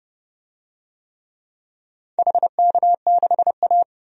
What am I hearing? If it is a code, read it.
Morse code, heard as a keyed tone, about 29 wpm, 700 Hz. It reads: HK6A
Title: DXCC-100 Call Signs - 4 Characters - 30wpm